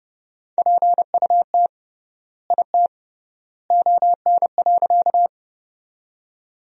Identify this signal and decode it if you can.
Morse code, heard as a keyed tone, about 30 wpm, 700 Hz. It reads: PUT IT ON.
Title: 3x Word-Spacing - Sentences from Top 200 Words - 30wpm